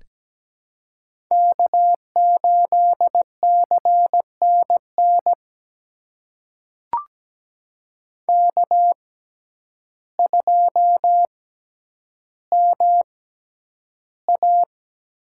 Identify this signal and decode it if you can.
Morse code, heard as a keyed tone, about 17 wpm, spaced as for 9 wpm, 700 Hz. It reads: K8CNN K2MA
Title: USA Call Signs - 5 Characters - 6x Character Spacing - 17wpm